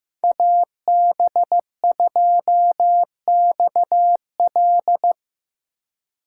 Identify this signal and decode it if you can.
Morse code, heard as a keyed tone, about 17 wpm, 700 Hz. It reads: AB2XL
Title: Speed Racing - USA 2x2 Call Signs 15wpm